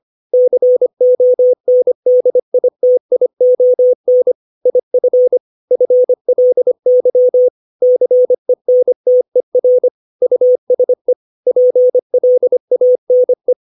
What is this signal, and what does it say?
Morse code, heard as a keyed tone, about 25 wpm, 500 Hz. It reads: CONDITION IF FLY CENTER USE PLANE
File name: 632 words / 632 Words 25WPM